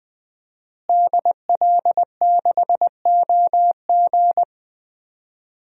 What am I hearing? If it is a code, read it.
Morse code, heard as a keyed tone, about 20 wpm, 700 Hz. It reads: DL6OG
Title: DXCC-100 Call Signs - 5 Characters - Repeated 2x - 20wpm